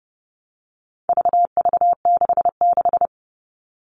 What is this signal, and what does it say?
Morse code, heard as a keyed tone, about 30 wpm, 700 Hz. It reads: VV66
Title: All Cut Numbers - 4-Digit 30wpm